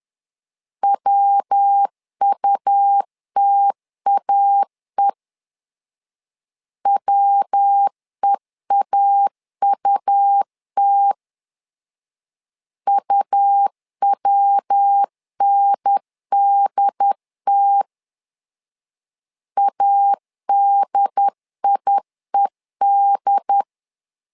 Morse code, heard as a keyed tone, about 10 wpm, 800 Hz. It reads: WUTAE WEAUT UWNDT ADIED
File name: LEZIONI / 3 MWRO / 3 MW 10wpm 1959